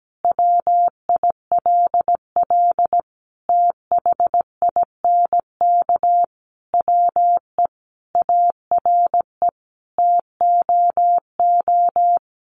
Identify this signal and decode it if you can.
Morse code, heard as a keyed tone, about 17 wpm, 700 Hz. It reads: WILL THINK WE ARE TOO
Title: Sentences from Top 1400 Words 17wpm